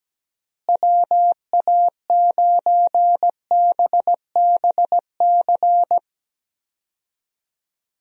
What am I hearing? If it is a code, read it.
Morse code, heard as a keyed tone, about 17 wpm, 700 Hz. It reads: WA9BBC
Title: USA Call Signs - 6 Characters - Repeated 2x - 17wpm